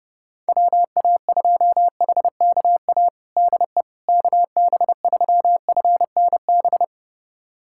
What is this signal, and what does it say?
Morse code, heard as a keyed tone, about 30 wpm, 700 Hz. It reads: WA2HKA DE KB3FNB
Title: USA Callsign DE Callsign - 6 Characters - 30wpm